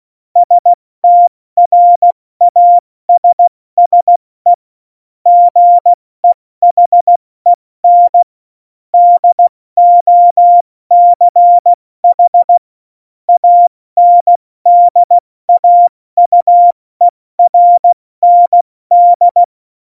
Morse code, heard as a keyed tone, about 16 wpm, 700 Hz. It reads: STRASSE GEHEN DOCH ANDAUERND
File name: Morsekurs / Witze / Witz 31. Oktober 17, Tempo 70